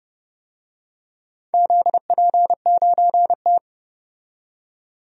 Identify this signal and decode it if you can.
Morse code, heard as a keyed tone, about 30 wpm, 700 Hz. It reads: ZP9T
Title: DXCC-100 Call Signs - 4 Characters - 30wpm